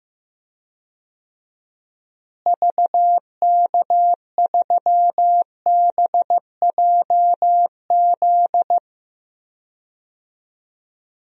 Morse code, heard as a keyed tone, about 15 wpm, 700 Hz. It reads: VK3BJZ
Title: DXCC-100 Call Signs - 6 Characters - Repeated 3x - 15wpm